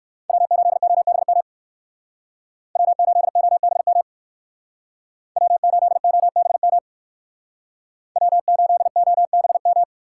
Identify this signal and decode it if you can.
Morse code, heard as a keyed tone, about 61 wpm, 700 Hz. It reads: W8OBM W8OBM W8OBM W8OBM